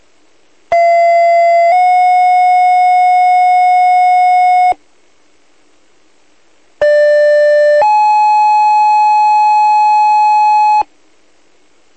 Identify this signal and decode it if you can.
Two-tone page, heard as a one-second tone, then a three-second tone, about 680 then 730 Hz, about 590 then 850 Hz. It is heard from 0.7 to 10.8 s.
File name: old-forge